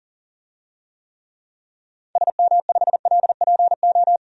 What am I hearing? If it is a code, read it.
Morse code, heard as a keyed tone, about 40 wpm, 700 Hz. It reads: SM5LPO
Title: DXCC-100 Call Signs - 6 Characters - 40wpm